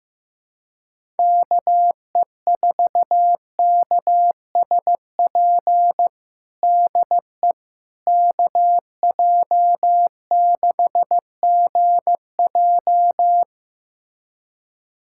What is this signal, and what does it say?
Morse code, heard as a keyed tone, about 15 wpm, 700 Hz. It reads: KE4KSP DE KJ6GJ
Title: USA Callsign DE Callsign 15wpm